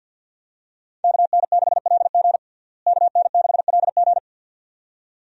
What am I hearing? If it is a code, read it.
Morse code, heard as a keyed tone, about 52 wpm, 700 Hz. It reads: KN6LG KN6LG
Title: USA Call Signs - 5 Characters - Repeated 2x - 50wpm